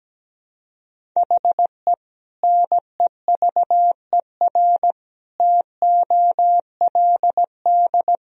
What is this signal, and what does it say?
Morse code, heard as a keyed tone, about 17 wpm, 700 Hz. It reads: HE NEVER TOLD
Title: Sentences from Top 400 Words and Abbreviations 17wpm